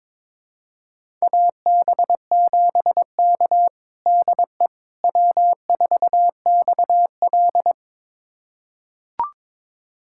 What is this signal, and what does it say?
Morse code, heard as a keyed tone, about 22 wpm, 700 Hz. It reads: AB7K DE W4XL
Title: USA Callsign DE Callsign - 4 Characters - 22wpm